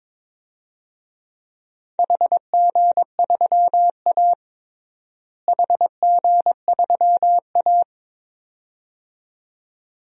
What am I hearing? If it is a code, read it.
Morse code, heard as a keyed tone, about 22 wpm, 700 Hz. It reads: HG3A HG3A
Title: DXCC-100 Call Signs - 4 Characters - Repeated 2x - 22wpm